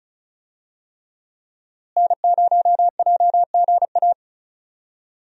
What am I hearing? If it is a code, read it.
Morse code, heard as a keyed tone, about 36 wpm, 700 Hz. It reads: N0JGA
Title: USA Call Signs - 5 Characters - Repeated 3x - 35wpm